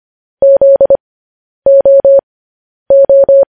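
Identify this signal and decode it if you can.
Morse code, heard as a keyed tone, about 25 wpm, spaced as for 10 wpm, 550 Hz. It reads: ZOO
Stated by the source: Morse transmission of news headlines.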